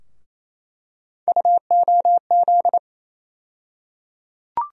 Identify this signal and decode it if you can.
Morse code, heard as a keyed tone, about 28 wpm, 700 Hz. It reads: UOZ
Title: USA Call Sign Postfixes - 3 Letters 28wpm